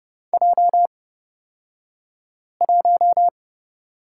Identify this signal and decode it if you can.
Morse code, heard as a keyed tone, about 30 wpm, spaced as for 5 wpm, 700 Hz. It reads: J1
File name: Lesson 36d - BT - Two Characters 30wpm